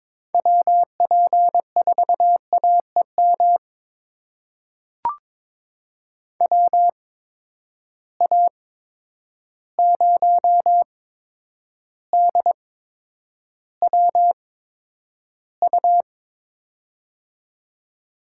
Morse code, heard as a keyed tone, about 22 wpm, spaced as for 9 wpm, 700 Hz. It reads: WP4AEM WA0DWU
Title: USA Call Signs - 6 Characters - 8x Character Spacing - 22wpm